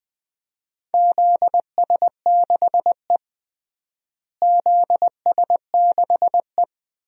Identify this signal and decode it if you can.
Morse code, heard as a keyed tone, about 20 wpm, 700 Hz. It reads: ZS6E ZS6E
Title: DXCC-100 Call Signs - 4 Characters - Repeated 2x - 20wpm